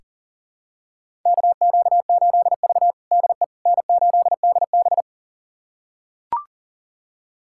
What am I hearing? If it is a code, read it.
Morse code, heard as a keyed tone, about 37 wpm, 700 Hz. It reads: KQ8V DE N8DB
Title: USA Callsign DE Callsign - 4 Characters - 40wpm